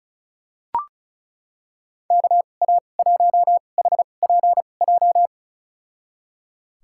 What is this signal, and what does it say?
Morse code, heard as a keyed tone, about 34 wpm, spaced as for 25 wpm, 700 Hz. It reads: KA1HPJ
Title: USA Call Signs - 6 Characters - 2x Character Spacing - 35wpm